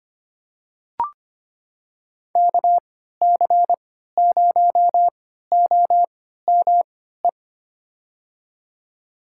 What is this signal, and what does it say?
Morse code, heard as a keyed tone, about 25 wpm, spaced as for 14 wpm, 700 Hz. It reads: KC0OME
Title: USA Call Signs - 6 Characters - 3x Character Spacing - 25wpm